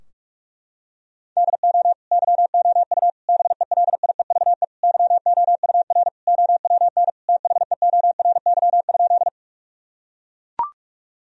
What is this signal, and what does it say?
Morse code, heard as a keyed tone, about 42 wpm, 700 Hz. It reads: DO YOU BELIEVE YOUR OWN THEORY?
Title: Sentences from Top 1000 Words - Review Part 3 - 45wpm